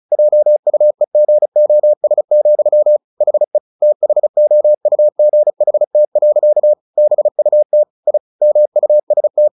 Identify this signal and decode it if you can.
Morse code, heard as a keyed tone, about 34 wpm, 600 Hz. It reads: JUEGOS, HE THOUGHT. BUT I MUST
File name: the old man and the sea / the old man and the sea 1 35wpm